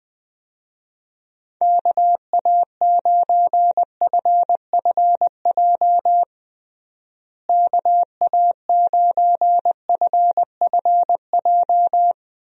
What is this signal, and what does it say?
Morse code, heard as a keyed tone, about 21 wpm, 700 Hz. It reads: KA9FFJ KA9FFJ
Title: USA Call Signs - 6 Characters - Repeated 3x - 20wpm